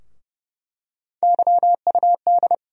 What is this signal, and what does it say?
Morse code, heard as a keyed tone, about 32 wpm, 700 Hz. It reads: YUD
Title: USA Call Sign Postfixes - 3 Letters 30wpm